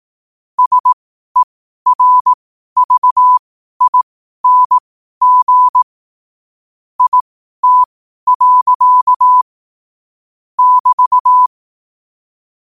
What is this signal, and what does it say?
Morse code, heard as a keyed tone, about 18 wpm, spaced as for 12 wpm, 1000 Hz. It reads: SERVING IT. <BT>
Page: 12 wpm morse code quotes